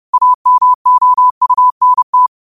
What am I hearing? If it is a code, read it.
Morse code, heard as a keyed tone, about 30 wpm, 1000 Hz. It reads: AMOUNT
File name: qotdfiles / 30wpm